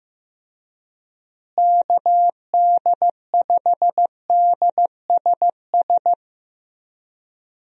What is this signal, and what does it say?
Morse code, heard as a keyed tone, about 15 wpm, 700 Hz. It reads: KD5DSS
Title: USA Call Signs - 6 Characters - Repeated 2x - 15wpm